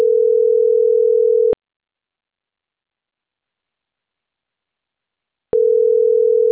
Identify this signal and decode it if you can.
Ringback tone (North America), 440 + 480 Hz.